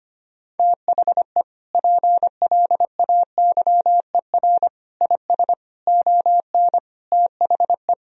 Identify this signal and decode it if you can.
Morse code, heard as a keyed tone, about 25 wpm, 700 Hz. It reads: THE PLAYER IS ON THE
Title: Sentences from Top 1000 Words - Review Part 4 - 25wpm